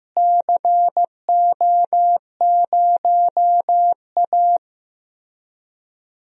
Morse code, heard as a keyed tone, about 15 wpm, 700 Hz. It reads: CO0A
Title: DXCC-100 Call Signs - 4 Characters - Repeated 2x - 15wpm